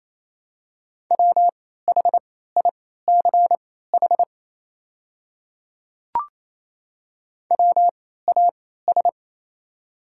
Morse code, heard as a keyed tone, about 28 wpm, spaced as for 16 wpm, 700 Hz. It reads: WHICH WAS
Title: 3x Character Spacing - Top 100 Words - 28wpm